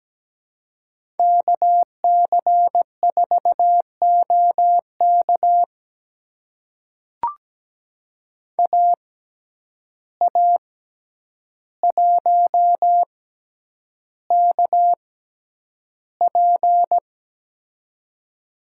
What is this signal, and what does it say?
Morse code, heard as a keyed tone, about 17 wpm, spaced as for 9 wpm, 700 Hz. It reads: KC4OK AA1KP
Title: USA Call Signs - 5 Characters - 6x Character Spacing - 17wpm